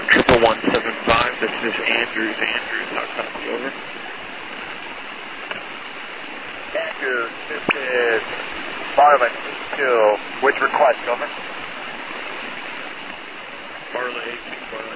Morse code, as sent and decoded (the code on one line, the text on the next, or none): none